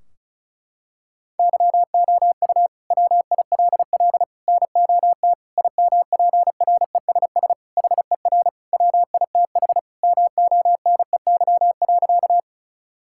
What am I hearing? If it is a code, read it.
Morse code, heard as a keyed tone, about 36 wpm, 700 Hz. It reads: YOU WILL NOT IMPRESS HER WITH MONEY.
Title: Sentences from Top 1600 Words 35wpm